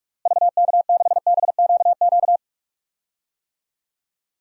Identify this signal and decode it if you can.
Morse code, heard as a keyed tone, about 46 wpm, 700 Hz. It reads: VK6BQQ
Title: DXCC-100 Call Signs - 6 Characters - Repeated 2x - 45wpm